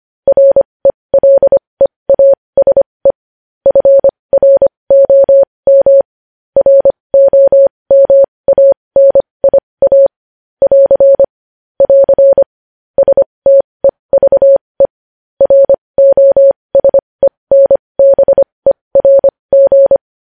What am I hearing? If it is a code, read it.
Morse code, heard as a keyed tone, about 25 wpm, spaced as for 20 wpm, 550 Hz. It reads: RELEASE FROM ROMANIA <AR> <AR> STEVE ROSENBERG
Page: News Headlines in Morse Code at 20 WPM